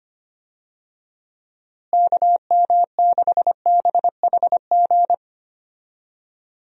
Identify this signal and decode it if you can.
Morse code, heard as a keyed tone, about 25 wpm, 700 Hz. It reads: KM6BHG
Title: USA Call Signs - 6 Characters - Repeated 3x - 25wpm